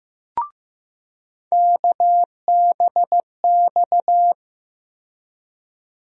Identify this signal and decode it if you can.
Morse code, heard as a keyed tone, about 15 wpm, 700 Hz. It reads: KBX
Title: USA Call Sign Postfixes - 3 Letters 15wpm